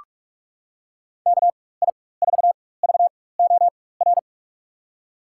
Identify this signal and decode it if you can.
Morse code, heard as a keyed tone, about 46 wpm, spaced as for 21 wpm, 700 Hz. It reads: KI4VOR
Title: USA Call Signs - 4x Character Spacing - 45wpm